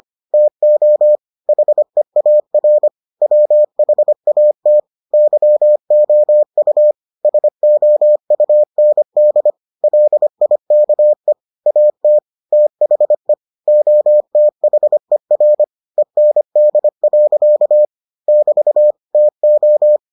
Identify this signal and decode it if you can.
Morse code, heard as a keyed tone, about 25 wpm, 600 Hz. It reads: TO HEAR WHAT YOU SOUND LIKE AT THE OTHER END. <BT> TO